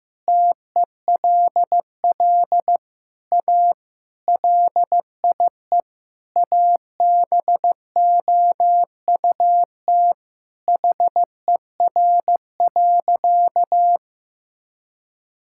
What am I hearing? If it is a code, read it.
Morse code, heard as a keyed tone, about 15 wpm, 700 Hz. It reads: TELL A LIE ABOUT HER.
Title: Sentences from Top 1600 Words 15wpm